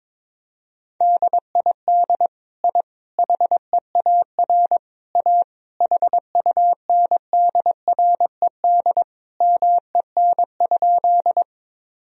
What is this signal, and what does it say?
Morse code, heard as a keyed tone, about 21 wpm, 700 Hz. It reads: DID I HEAR A HUNDRED MEN?
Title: Sentences from Top 1000 Words - Review Part 2 - 22wpm